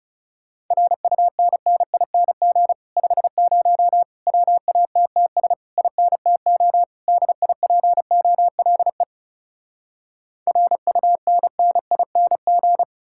Morse code, heard as a keyed tone, about 40 wpm, 700 Hz. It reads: RUNNING 50 WATTS INTO DIPOLE RUNNING